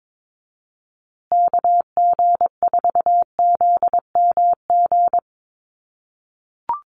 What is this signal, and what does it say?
Morse code, heard as a keyed tone, about 22 wpm, 700 Hz. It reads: KG4ZMG
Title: USA Call Signs Repeated 2x 22wpm